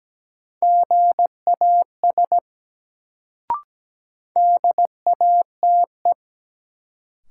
Morse code, heard as a keyed tone, about 17 wpm, 700 Hz. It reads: GAS DATE
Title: Sentences from Top 1100 Words - New Word Drill Alt - 17wpm